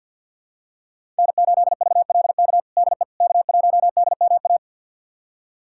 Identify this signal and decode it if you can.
Morse code, heard as a keyed tone, about 48 wpm, 700 Hz. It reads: N8VLK DE K1DMA
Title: USA Callsign DE Callsign - 5 Characters - 50wpm